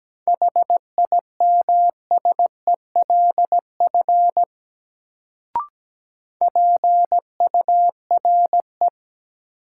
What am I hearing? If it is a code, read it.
Morse code, heard as a keyed tone, about 17 wpm, 700 Hz. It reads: HIMSELF PURE
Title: Sentences from Top 1500 Words - New Word Drill Alt - 17wpm